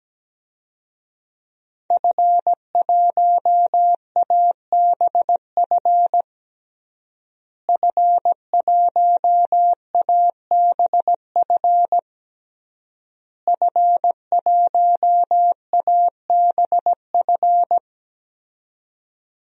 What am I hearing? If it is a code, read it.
Morse code, heard as a keyed tone, about 17 wpm, 700 Hz. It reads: F1ABF F1ABF F1ABF
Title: DXCC-100 Call Signs - 5 Characters - Repeated 3x - 17wpm